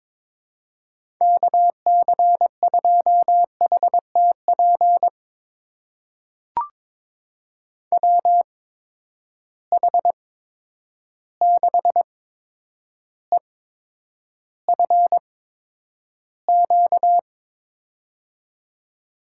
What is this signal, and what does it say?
Morse code, heard as a keyed tone, about 22 wpm, spaced as for 9 wpm, 700 Hz. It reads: KC2HTP WH6EFQ
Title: USA Call Signs - 6 Characters - 8x Character Spacing - 22wpm